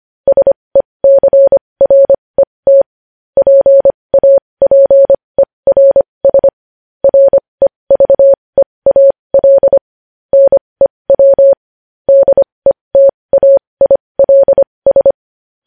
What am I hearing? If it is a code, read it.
Morse code, heard as a keyed tone, about 25 wpm, spaced as for 20 wpm, 550 Hz. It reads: SECRET PAPERS REVEAL NEW DETAILS
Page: News headlines in morse code at 20 wpm